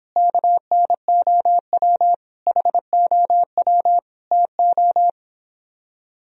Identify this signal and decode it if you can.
Morse code, heard as a keyed tone, about 26 wpm, 700 Hz. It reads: KNOW HOW TO